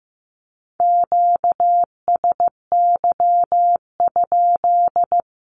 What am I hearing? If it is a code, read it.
Morse code, heard as a keyed tone, about 14 wpm, 700 Hz. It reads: QSY?
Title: Q-Codes Repeated 3x 15wpm